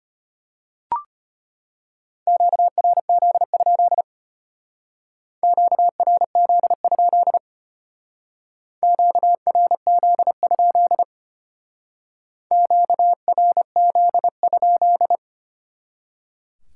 Morse code, heard as a keyed tone, about 33 wpm, 700 Hz. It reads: QRZ? QRZ? QRZ? QRZ?